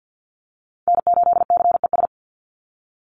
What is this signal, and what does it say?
Morse code, heard as a keyed tone, about 50 wpm, 700 Hz. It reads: N8CES
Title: USA Call Signs 50wpm